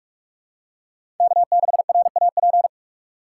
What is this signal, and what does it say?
Morse code, heard as a keyed tone, about 46 wpm, 700 Hz. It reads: K6RAP
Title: USA Call Signs - 5 Characters - 3x Character Spacing - 45wpm